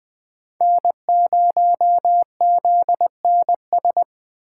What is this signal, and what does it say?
Morse code, heard as a keyed tone, about 20 wpm, 700 Hz. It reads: N0ZNS
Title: USA Call Signs - 5 Characters - 4x Character Spacing - 20wpm